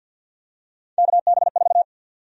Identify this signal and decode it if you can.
Morse code, heard as a keyed tone, about 52 wpm, 700 Hz. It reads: KB4
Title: USA Call Sign Prefixes - 50wpm